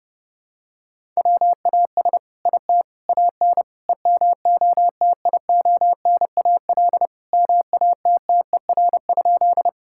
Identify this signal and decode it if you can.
Morse code, heard as a keyed tone, about 30 wpm, 700 Hz. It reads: WAS IT AN EMOTIONAL MATTER?
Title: Sentences from Top 1000 Words - Review Part 5 - 30wpm